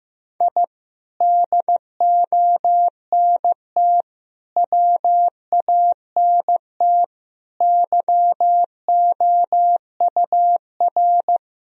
Morse code, heard as a keyed tone, about 15 wpm, 700 Hz. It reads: I DONT WANT YOUR
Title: Sentences from Top 1000 Words - Review Part 3 - 15wpm